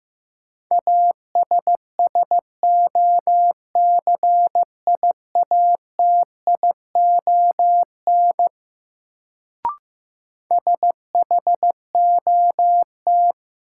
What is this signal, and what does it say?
Morse code, heard as a keyed tone, about 15 wpm, 700 Hz. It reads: ASSOCIATION SHOT
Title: Sentences from Top 1100 Words - New Word Drill Alt - 15wpm